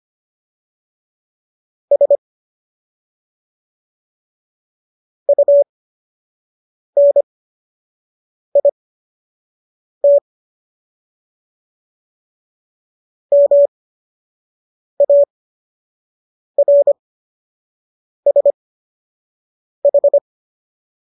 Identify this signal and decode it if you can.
Morse code, heard as a keyed tone, about 25 wpm, spaced as for 6 wpm, 570 Hz. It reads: S UNIT MARSH